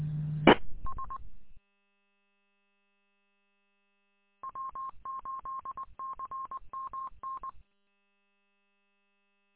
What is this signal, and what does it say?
Morse code, heard as a keyed tone, about 20 wpm, 1070 Hz. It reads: E S W8CMN
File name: w8cmn- -combined